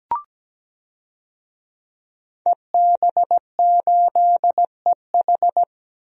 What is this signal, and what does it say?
Morse code, heard as a keyed tone, about 17 wpm, 700 Hz. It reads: EB8EH
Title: DXCC-100 Call Signs - 5 Characters - 17wpm